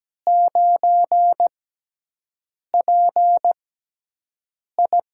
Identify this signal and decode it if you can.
Morse code, heard as a keyed tone, about 17 wpm, spaced as for 6 wpm, 700 Hz. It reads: 9PI
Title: USA Call Signs - 6x Character Spacing - 17wpm